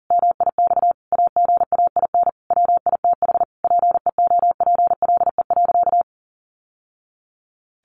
Morse code, heard as a keyed tone, about 40 wpm, 700 Hz. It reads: MIX AGAIN WITH PEOPLE.
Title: Sentences from Top 1100 Words 40wpm